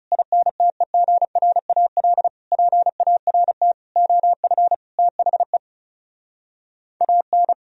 Morse code, heard as a keyed tone, about 40 wpm, 700 Hz. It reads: INTEGRAL PART OF THE AN